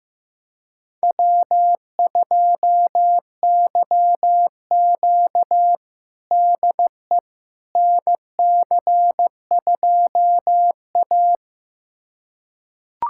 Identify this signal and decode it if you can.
Morse code, heard as a keyed tone, about 15 wpm, 700 Hz. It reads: W2YQ DE NC2A
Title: USA Callsign DE Callsign - 4 Characters - 15wpm